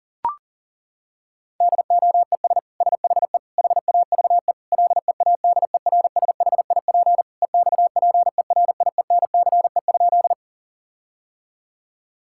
Morse code, heard as a keyed tone, about 40 wpm, 700 Hz. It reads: DOES SHE HAVE LEADERSHIP EXPERIENCE?
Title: Sentences from Top 1600 Words 40wpm